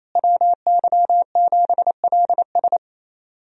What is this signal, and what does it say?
Morse code, heard as a keyed tone, about 28 wpm, 700 Hz. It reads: WY7LS
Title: USA Call Signs - 5 Characters - Repeated 2x - 28wpm